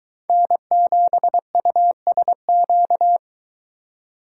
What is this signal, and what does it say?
Morse code, heard as a keyed tone, about 23 wpm, 700 Hz. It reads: N7USQ